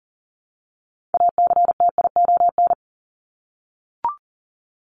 Morse code, heard as a keyed tone, about 40 wpm, 700 Hz. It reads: ACTION
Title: Top 501-750 Words 40wpm